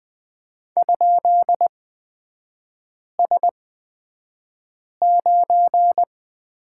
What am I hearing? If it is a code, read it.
Morse code, heard as a keyed tone, about 20 wpm, spaced as for 5 wpm, 690 Hz. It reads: ?S9
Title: 236 Lesson 38e - Letter q - Three Characters 20wpm